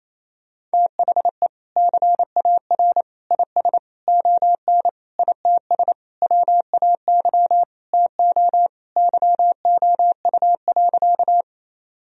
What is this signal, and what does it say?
Morse code, heard as a keyed tone, about 28 wpm, 700 Hz. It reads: THE CAR IS ON ITS WAY TO YOU.
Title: Sentences from Top 1000 Words - Review Part 1 - 28wpm